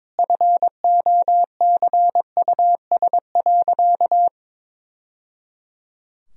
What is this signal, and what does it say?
Morse code, heard as a keyed tone, about 22 wpm, 700 Hz. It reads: FOCUS.
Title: Sentences from Top 1000 Words - Review Part 4 - 22wpm